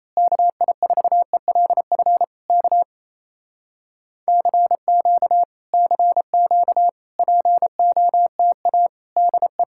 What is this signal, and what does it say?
Morse code, heard as a keyed tone, about 26 wpm, 700 Hz. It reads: KI4ELF K CQ CQ POTA DE
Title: Speed-Racing Common QSO Phrases - 28wpm